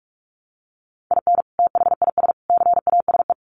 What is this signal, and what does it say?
Morse code, heard as a keyed tone, about 44 wpm, 700 Hz. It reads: IN THIS CASE
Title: Sets of 3 Words 45wpm